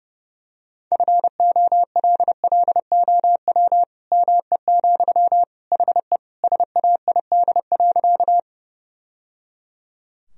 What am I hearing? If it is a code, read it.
Morse code, heard as a keyed tone, about 30 wpm, 700 Hz. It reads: FOLLOW ME, HE SAID.
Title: Sentences from Top 1000 Words - Review Part 1 - 30wpm